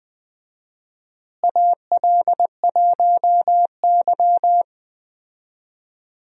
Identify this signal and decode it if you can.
Morse code, heard as a keyed tone, about 20 wpm, 700 Hz. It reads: AL1Y
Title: USA Call Signs - 4 Characters - Repeated 3x - 20wpm